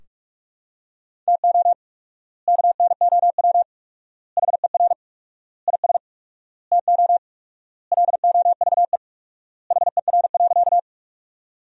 Morse code, heard as a keyed tone, about 42 wpm, 700 Hz. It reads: TO KNOW HER IS TO LOVE HER.
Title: 4x Word-Spacing - Sentences from Top 400 Words - 45wpm